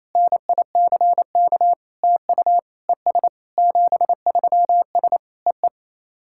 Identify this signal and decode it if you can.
Morse code, heard as a keyed tone, about 28 wpm, 700 Hz. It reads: NICK TU ES 73S EE
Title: SOTA QSO Practice 28wpm